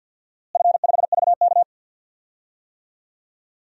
Morse code, heard as a keyed tone, about 53 wpm, 700 Hz. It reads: F5VK
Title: DXCC-100 Call Signs - 4 Characters - Repeated 2x - 50wpm